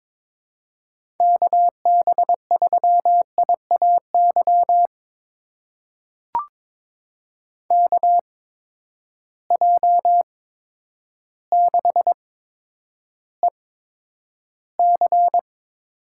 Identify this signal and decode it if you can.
Morse code, heard as a keyed tone, about 22 wpm, spaced as for 10 wpm, 700 Hz. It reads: KB3IAY KJ6EC
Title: USA Call Signs - 6 Characters - 8x Character Spacing - 22wpm